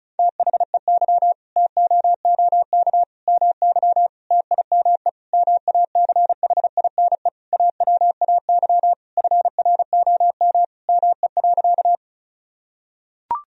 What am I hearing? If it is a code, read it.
Morse code, heard as a keyed tone, about 36 wpm, 700 Hz. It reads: THEY TOOK MY TIME MACHINE AWAY FROM ME.
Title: Sentences from Top 1000 Words - Review Part 2 - 35wpm